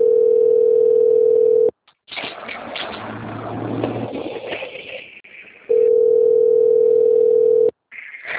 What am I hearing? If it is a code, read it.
Ringback tone (North America), 440 + 480 Hz.